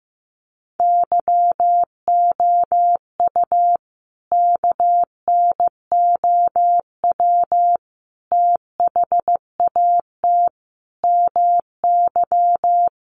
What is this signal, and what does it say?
Morse code, heard as a keyed tone, about 15 wpm, 700 Hz. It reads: YOU KNOW THAT MY
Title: Sentences from Top 1400 Words 15wpm